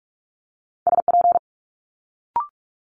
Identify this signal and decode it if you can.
Morse code, heard as a keyed tone, about 46 wpm, 700 Hz. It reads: SP
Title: Multi-letter Phonemes Quick Fire 45wpm